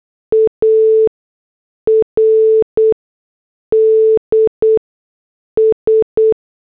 Morse code, heard as a keyed tone, about 8 wpm, spaced as for 6 wpm, 440 Hz. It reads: ARDS